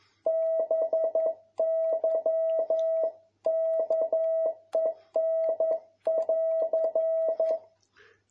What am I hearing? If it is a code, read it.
Morse code, heard as a keyed tone, about 11 wpm, 650 Hz. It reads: BY KEN<AR>
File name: jh8cbh / 2021cw